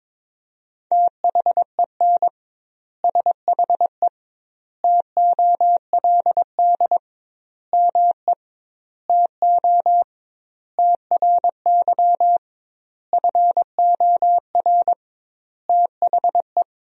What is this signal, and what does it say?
Morse code, heard as a keyed tone, about 22 wpm, 700 Hz. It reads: THEN SHE TOLD ME TO TRY FOR THE
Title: 2x Word-Spacing - Sentences from Top 400 Words - 22wpm